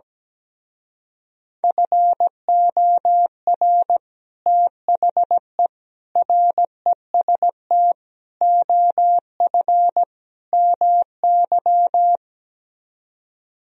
Morse code, heard as a keyed tone, about 17 wpm, 700 Hz. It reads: FOR THE REST OF MY
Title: Speed-Racing - Sets of 5 Words 17wpm